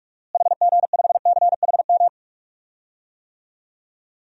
Morse code, heard as a keyed tone, about 41 wpm, 700 Hz. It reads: HG5CHM
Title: DXCC-100 Call Signs - 6 Characters - Repeated 3x - 45wpm